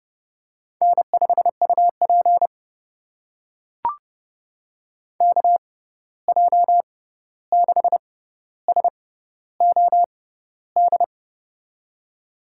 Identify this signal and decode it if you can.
Morse code, heard as a keyed tone, about 30 wpm, spaced as for 14 wpm, 700 Hz. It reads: N5UP KJ6SOD
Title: USA Call Signs - 6x Character Spacing - 30wpm